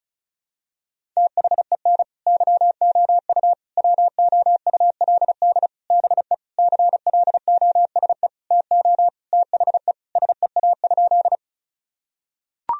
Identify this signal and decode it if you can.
Morse code, heard as a keyed tone, about 36 wpm, 700 Hz. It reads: THEN YOU WOULD BE CLOSE TO THE SEA?
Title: Sentences from Top 1000 Words - Review Part 1 - 35wpm